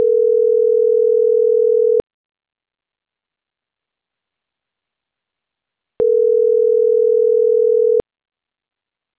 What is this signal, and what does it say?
Ringback tone (North America), 440 + 480 Hz.